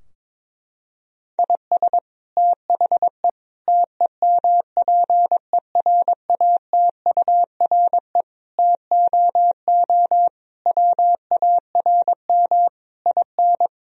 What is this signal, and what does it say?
Morse code, heard as a keyed tone, about 22 wpm, 700 Hz. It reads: IS THE TEMPERATURE TOO WARM IN
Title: Sentences from Top 1000 Words - Review Part 4 - 22wpm